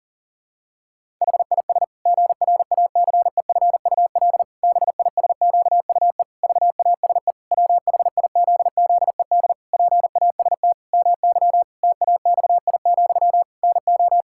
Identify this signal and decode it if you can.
Morse code, heard as a keyed tone, about 37 wpm, 700 Hz. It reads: HIS GRACEFUL BISQUE VASE WHIZZED PAST MY TAXI, NO
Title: Pangram Challenge v2 - 40wpm